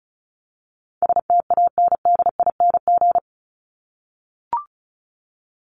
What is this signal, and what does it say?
Morse code, heard as a keyed tone, about 34 wpm, 700 Hz. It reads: STANDING
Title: ING Suffix - Top 100 Words with Base Words 35wpm